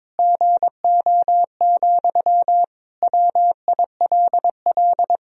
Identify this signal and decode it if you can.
Morse code, heard as a keyed tone, about 21 wpm, 700 Hz. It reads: GO, WILL
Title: Sentences from Top 1000 Words - Review Part 2 - 22wpm